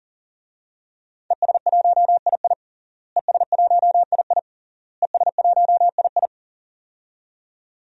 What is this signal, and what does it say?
Morse code, heard as a keyed tone, about 40 wpm, 700 Hz. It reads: ES1II ES1II ES1II
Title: DXCC-100 Call Signs - 5 Characters - Repeated 3x - 40wpm